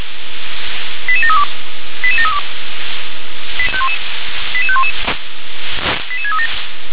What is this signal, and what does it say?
Five-tone selective call (ZVEI), heard as quick runs of five tones, tones of about 70 ms. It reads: 88732 88732 88511 88637
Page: ZVEI